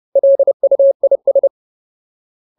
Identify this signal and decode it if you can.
Morse code, heard as a keyed tone, about 30 wpm, 550 Hz. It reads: LUIS
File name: Ultimate Word List / 30WPM Ultimate